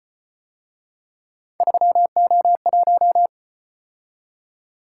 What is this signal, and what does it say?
Morse code, heard as a keyed tone, about 30 wpm, 700 Hz. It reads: 3O1
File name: Speed-Racing - Three Letter-Number Combinations - 25wpm